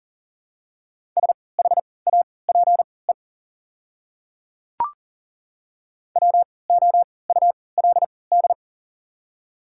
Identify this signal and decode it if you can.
Morse code, heard as a keyed tone, about 40 wpm, spaced as for 23 wpm, 700 Hz. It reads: SHAPE WOULD
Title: 3x Character Spacing - Top 500 Words - 40wpm